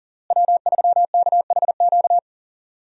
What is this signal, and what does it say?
Morse code, heard as a keyed tone, about 40 wpm, 700 Hz. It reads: W3KHQ
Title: USA Call Signs - 5 Characters - Repeated 2x - 40wpm